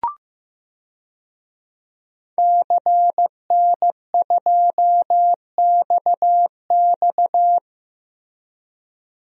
Morse code, heard as a keyed tone, about 15 wpm, 700 Hz. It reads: CN2XX
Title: DXCC-100 Call Signs - 5 Characters - Repeated 3x - 15wpm